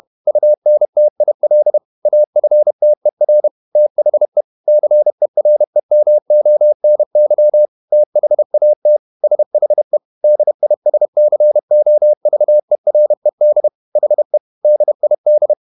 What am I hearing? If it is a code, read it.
Morse code, heard as a keyed tone, about 31 wpm, 600 Hz. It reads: UNTIL AFTER THE CEREMONY THAT SHE DISCOVERED HE DID
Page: Never 31wpm | CW med Gnister